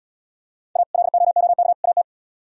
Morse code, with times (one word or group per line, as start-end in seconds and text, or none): 0.8-0.8 A
0.9-1.7 LOOK
1.8-2.0 AT